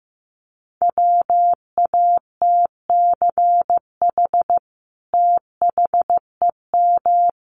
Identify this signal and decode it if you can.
Morse code, heard as a keyed tone, about 15 wpm, 700 Hz. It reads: WATCH THEM
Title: Sentences from Top 1400 Words 15wpm